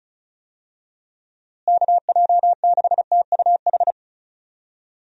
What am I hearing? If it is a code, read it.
Morse code, heard as a keyed tone, about 36 wpm, 700 Hz. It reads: KJ6TUH
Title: USA Call Signs - 6 Characters - Repeated 3x - 35wpm